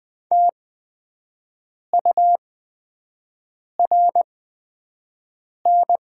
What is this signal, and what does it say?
Morse code, heard as a keyed tone, about 20 wpm, spaced as for 5 wpm, 700 Hz. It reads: TURN
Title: 8x Character Spacing - Top 200 Words - 20wpm